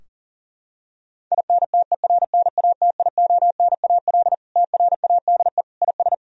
Morse code, heard as a keyed tone, about 37 wpm, 700 Hz. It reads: INTERNATIONAL TRADE IS
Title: Sentences from Top 1000 Words - Review Part 4 - 40wpm